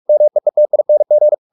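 Morse code, heard as a keyed tone, about 44 wpm, 600 Hz. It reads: MEETING
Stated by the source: Morse practice at 45 wpm, the novel Never by Ken Follett in English, from CW med Gnister.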